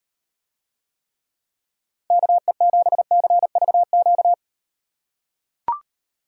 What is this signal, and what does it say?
Morse code, heard as a keyed tone, about 38 wpm, 700 Hz. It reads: KE7CVQ